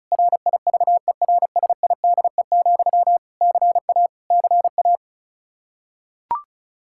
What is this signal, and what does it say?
Morse code, heard as a keyed tone, about 34 wpm, 700 Hz. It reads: RIVERSIDE, CA CA
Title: Practice QSO - Exchange 1 - Beginning - 35wpm